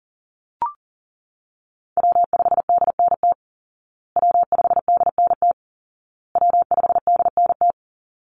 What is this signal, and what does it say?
Morse code, heard as a keyed tone, about 37 wpm, 700 Hz. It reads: W5DNT W5DNT W5DNT
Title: Call Signs Repeated 2x - 40wpm